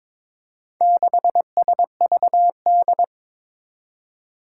Morse code, heard as a keyed tone, about 22 wpm, 700 Hz. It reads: 6SVD
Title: Four Challenging Letter-Number Combinations 22wpm